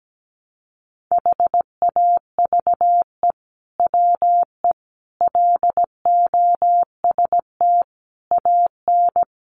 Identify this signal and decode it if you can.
Morse code, heard as a keyed tone, about 17 wpm, 700 Hz. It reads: HAVE WE LOST AN
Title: Sentences from Top 1400 Words 17wpm